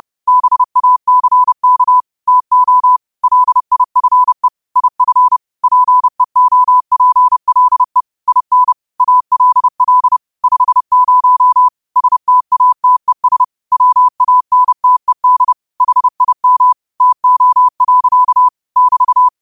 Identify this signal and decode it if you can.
Morse code, heard as a keyed tone, about 30 wpm, 1000 Hz. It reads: BACK TO LIFE IF PEOPLE IN ALL 50 STATES WANTED HIM TO. <BT>